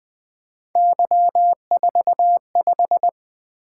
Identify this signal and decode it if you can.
Morse code, heard as a keyed tone, about 20 wpm, 700 Hz. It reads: Y45
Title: Three Letter-Number Combinations - 20wpm